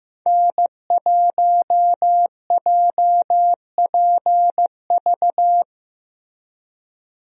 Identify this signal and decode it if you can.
Morse code, heard as a keyed tone, about 15 wpm, 700 Hz. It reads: N1JPV
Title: USA Call Signs - 5 Characters - Repeated 3x - 15wpm